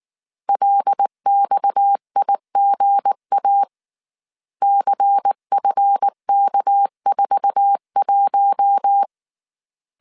Morse code, heard as a keyed tone, about 19 wpm, 800 Hz. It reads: LXIGA CFK41
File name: CW / DET / 1 9 1500C S A 18wpm 2459